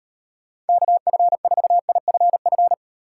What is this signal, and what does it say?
Morse code, heard as a keyed tone, about 38 wpm, 700 Hz. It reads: KF4IFF